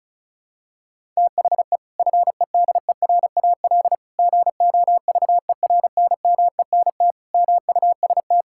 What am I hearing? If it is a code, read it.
Morse code, heard as a keyed tone, about 36 wpm, 700 Hz. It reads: THE FEDERAL GOVERNMENT MUST
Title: Sentences from Top 1000 Words - Review Part 5 - 35wpm